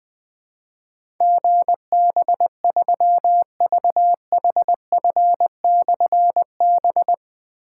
Morse code, heard as a keyed tone, about 20 wpm, 700 Hz. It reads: GB3VHF/B
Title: Difficult Call Signs 20wpm